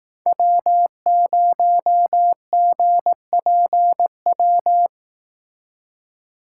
Morse code, heard as a keyed tone, about 18 wpm, 700 Hz. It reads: W0GPW
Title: Speed Racing - USA 1x3 Call Signs 15wpm